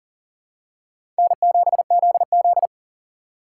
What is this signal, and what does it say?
Morse code, heard as a keyed tone, about 40 wpm, 700 Hz. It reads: N7ZZ
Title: USA Call Signs - 3x Character Spacing - 40wpm